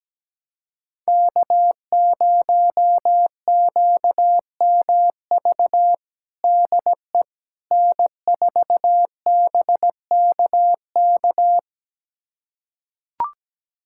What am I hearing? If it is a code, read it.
Morse code, heard as a keyed tone, about 17 wpm, 700 Hz. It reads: K0QMV DE N4BKK
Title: USA Callsign DE Callsign - 5 Characters - 17wpm